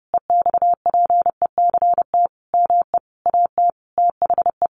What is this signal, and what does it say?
Morse code, heard as a keyed tone, about 30 wpm, 700 Hz. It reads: EXPECT ME AT THE
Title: Sentences from Top 1100 Words 30wpm